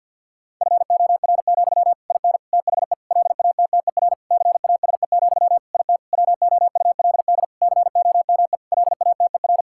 Morse code, heard as a keyed tone, about 50 wpm, 700 Hz. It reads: FOR, IN THE LATTER CASE, IT WOULD COME LATER
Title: Sentences from Top 1600 Words 50wpm